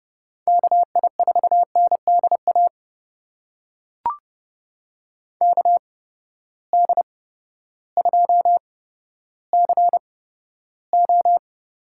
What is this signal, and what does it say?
Morse code, heard as a keyed tone, about 30 wpm, spaced as for 14 wpm, 700 Hz. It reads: KI4NDA KD2CO
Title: USA Call Signs - 6 Characters - 8x Character Spacing - 30wpm